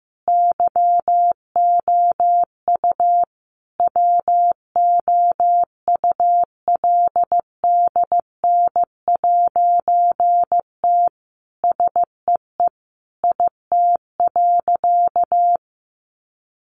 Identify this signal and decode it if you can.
Morse code, heard as a keyed tone, about 15 wpm, 700 Hz. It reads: YOU WOULDN'T SEE IT.
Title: A Sound Of Thunder 15wpm